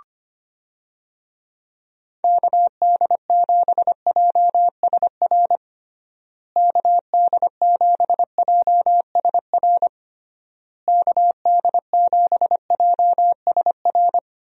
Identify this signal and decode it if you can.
Morse code, heard as a keyed tone, about 25 wpm, 700 Hz. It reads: KD7JSR KD7JSR KD7JSR
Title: USA Call Signs - 6 Characters - Repeated 3x - 25wpm